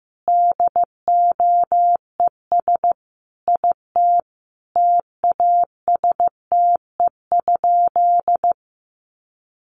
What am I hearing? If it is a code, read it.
Morse code, heard as a keyed tone, about 15 wpm, 700 Hz. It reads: DOES IT TASTE?
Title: Sentences from Top 1100 Words 15wpm